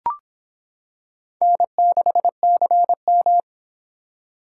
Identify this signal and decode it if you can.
Morse code, heard as a keyed tone, about 25 wpm, 700 Hz. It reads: N6CM